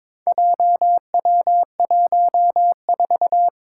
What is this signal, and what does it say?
Morse code, heard as a keyed tone, about 22 wpm, 700 Hz. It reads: JW14
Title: Four Challenging Letter-Number Combinations 22wpm